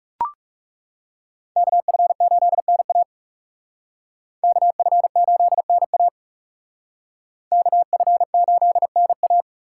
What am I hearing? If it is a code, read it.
Morse code, heard as a keyed tone, about 38 wpm, 700 Hz. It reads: KF8NA KF8NA KF8NA